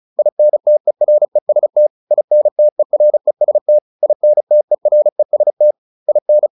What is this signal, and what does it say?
Morse code, heard as a keyed tone, about 34 wpm, 600 Hz. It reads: INTEREST INTEREST INTEREST IN